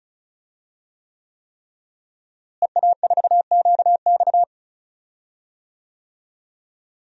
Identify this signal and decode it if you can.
Morse code, heard as a keyed tone, about 34 wpm, 700 Hz. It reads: EA4QX
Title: DXCC-100 Call Signs - 5 Characters - Repeated 3x - 35wpm